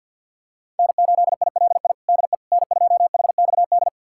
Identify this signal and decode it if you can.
Morse code, heard as a keyed tone, about 50 wpm, 690 Hz. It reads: N8ILI DE N2HXD
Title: USA Callsign DE Callsign - 5 Characters - 50wpm